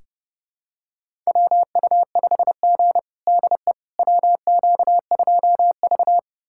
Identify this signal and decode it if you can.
Morse code, heard as a keyed tone, about 32 wpm, 700 Hz. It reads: WU5G DE WQ2V
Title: USA Callsign DE Callsign - 4 Characters - 30wpm